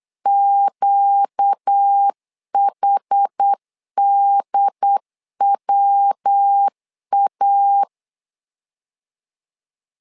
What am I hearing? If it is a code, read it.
Morse code, heard as a keyed tone, about 8 wpm, 800 Hz. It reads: QHDWA
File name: LEZIONI / 4 Q1 8wpm 3035